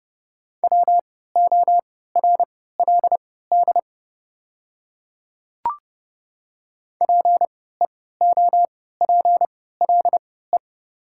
Morse code, heard as a keyed tone, about 28 wpm, spaced as for 17 wpm, 690 Hz. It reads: WORLD PEOPLE
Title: 3x Character Spacing - Top 200 Words - 30wpm